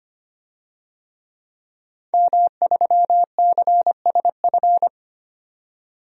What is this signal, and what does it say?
Morse code, heard as a keyed tone, about 25 wpm, 700 Hz. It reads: M3CSF
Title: DXCC-100 Call Signs - 5 Characters - 25wpm